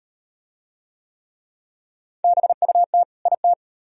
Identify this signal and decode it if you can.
Morse code, heard as a keyed tone, about 38 wpm, 700 Hz. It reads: BUT IT